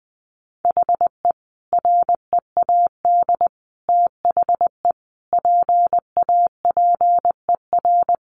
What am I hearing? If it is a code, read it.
Morse code, heard as a keyed tone, about 20 wpm, 700 Hz. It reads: HE READ THE PAPER
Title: Sentences from Top 1100 Words 20wpm